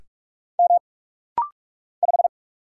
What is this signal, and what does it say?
Morse code, heard as a keyed tone, about 45 wpm, 700 Hz. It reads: M 5
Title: Single Letter-Number Alt - Rapid-Fire 45wpm